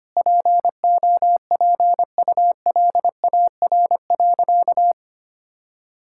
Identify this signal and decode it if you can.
Morse code, heard as a keyed tone, about 25 wpm, 700 Hz. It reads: POPULAR.
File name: practice / 25wpm / Sentences with 3-Word Phrases - 25wpm